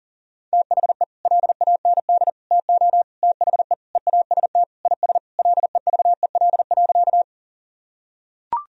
Morse code, heard as a keyed tone, about 40 wpm, 700 Hz. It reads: THE LAND TO THE EAST IS LEVEL.
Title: Sentences from Top 1000 Words - Review Part 4 - 40wpm